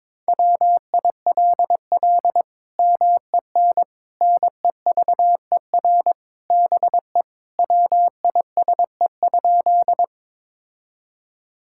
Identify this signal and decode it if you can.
Morse code, heard as a keyed tone, about 21 wpm, 700 Hz. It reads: WILL MEN NEVER BE WISE?
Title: Sentences from Top 1600 Words 22wpm